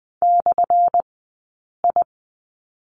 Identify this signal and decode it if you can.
Morse code, heard as a keyed tone, about 20 wpm, spaced as for 8 wpm, 700 Hz. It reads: /I
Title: 190 Lesson 31d - Letter v - Two Characters 20wpm